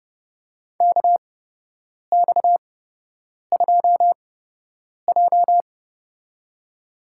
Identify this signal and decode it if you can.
Morse code, heard as a keyed tone, about 30 wpm, spaced as for 8 wpm, 700 Hz. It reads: KX2J
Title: USA Call Signs - 4 Characters - 8x Character Spacing - 30wpm